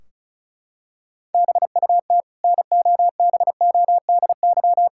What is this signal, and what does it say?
Morse code, heard as a keyed tone, about 36 wpm, 700 Hz. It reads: BUT NOBODY
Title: Sentences from Top 1600 Words 35wpm